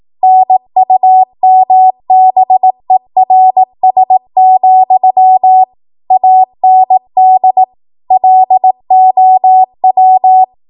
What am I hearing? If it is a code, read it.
Morse code, heard as a keyed tone, about 18 wpm, 750 Hz. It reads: NUMBERS, AND LOW
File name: Archive / 18 WPM / 18WPM